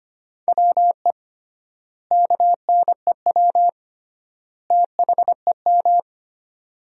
Morse code, heard as a keyed tone, about 25 wpm, 700 Hz. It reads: WE KNEW THEM
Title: 3x Word-Spacing - Sentences from Top 400 Words - 25wpm